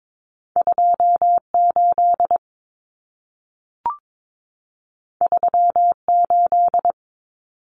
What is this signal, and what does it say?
Morse code, heard as a keyed tone, about 22 wpm, 700 Hz. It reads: 28 38
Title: Common Cut Numbers - 2-Digit 22wpm